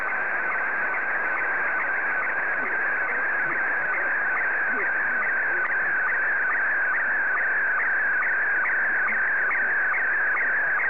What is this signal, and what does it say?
SSTV (slow-scan television), Scottie 1.